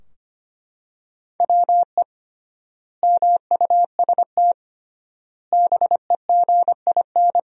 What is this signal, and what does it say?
Morse code, heard as a keyed tone, about 25 wpm, 700 Hz. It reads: WE MUST BEGIN
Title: 3x Word-Spacing - Sentences from Top 300 Words - 25wpm